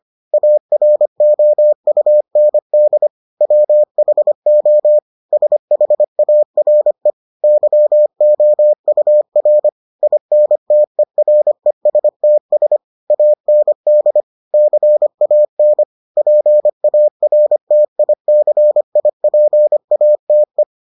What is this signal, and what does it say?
Morse code, heard as a keyed tone, about 25 wpm, 600 Hz. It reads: AROUND WHO SHARE YOUR INTERESTS AND CAN PARTICIPATE